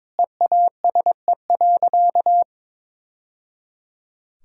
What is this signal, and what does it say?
Morse code, heard as a keyed tone, about 22 wpm, 700 Hz. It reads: EASE.
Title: Sentences from Top 1000 Words - Review Part 1 - 22wpm